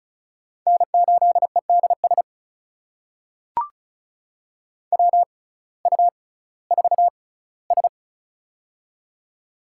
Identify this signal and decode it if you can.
Morse code, heard as a keyed tone, about 36 wpm, spaced as for 19 wpm, 700 Hz. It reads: N8EDS WU4S
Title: USA Call Signs - 6x Character Spacing - 35wpm